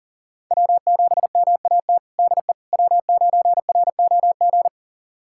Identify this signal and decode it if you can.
Morse code, heard as a keyed tone, about 40 wpm, 700 Hz. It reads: W7MAT DE W9ROG
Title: USA Callsign DE Callsign - 5 Characters - 40wpm